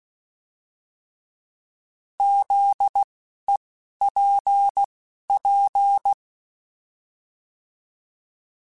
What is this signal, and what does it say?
Morse code, heard as a keyed tone, about 16 wpm, spaced as for 11 wpm, 800 Hz. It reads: ZEPP
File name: fromSMs / FromSMs mp3s / Antennas@10